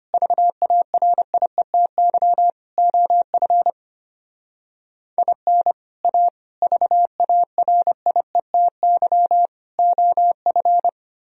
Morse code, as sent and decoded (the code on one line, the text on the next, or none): ...- .- .-. .. - -.-- / --- ..-. / .. -. / .- / ...- .- .-. .. - -.-- / --- ..-.
VARIETY OF IN A VARIETY OF